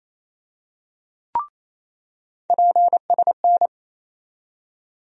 Morse code, heard as a keyed tone, about 28 wpm, 700 Hz. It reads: PSN
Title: USA Call Sign Postfixes - 3 Letters 28wpm